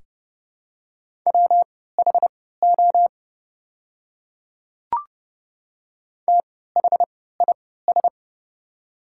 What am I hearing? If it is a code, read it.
Morse code, heard as a keyed tone, about 30 wpm, spaced as for 17 wpm, 700 Hz. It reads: WHO THIS
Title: 3x Character Spacing - Top 200 Words - 30wpm